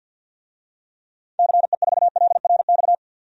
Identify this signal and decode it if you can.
Morse code, heard as a keyed tone, about 53 wpm, 700 Hz. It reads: CE4LRX
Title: DXCC-100 Call Signs - 6 Characters - 50wpm